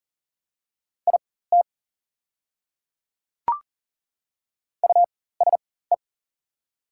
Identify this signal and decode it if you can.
Morse code, heard as a keyed tone, about 42 wpm, spaced as for 19 wpm, 700 Hz. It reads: IT USE
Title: 4x Character Spacing - Top 100 Words - 40wpm